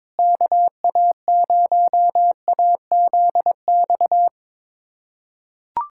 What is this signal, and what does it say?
Morse code, heard as a keyed tone, about 22 wpm, 700 Hz. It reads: KA0AZX
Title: USA Callsign DE Callsign 22wpm